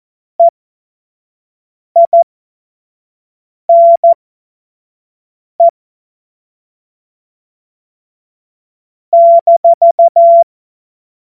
Morse code, heard as a keyed tone, about 14 wpm, spaced as for 5 wpm, 680 Hz. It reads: EINE -
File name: Morsekurs / Witze / Witz 04. Oktober 2022 25 F70